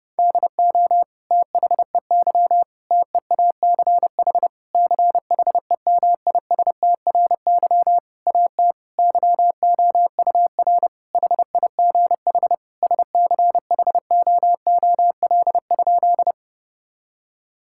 Morse code, heard as a keyed tone, about 30 wpm, 700 Hz. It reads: DO THEY TEACH CHEMISTRY AT YOUR HIGH SCHOOL?
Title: Sentences from Top 1000 Words - Review Part 6 - 30wpm